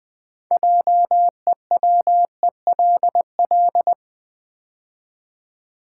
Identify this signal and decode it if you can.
Morse code, heard as a keyed tone, about 20 wpm, 700 Hz. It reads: JEWELL
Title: Build A Word - 3 Letters At A Time - 9-Letter Words - 20wpm